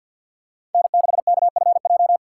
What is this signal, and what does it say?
Morse code, heard as a keyed tone, about 53 wpm, 700 Hz. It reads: N6KFJ
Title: USA Call Signs - 5 Characters - 6x Character Spacing - 50wpm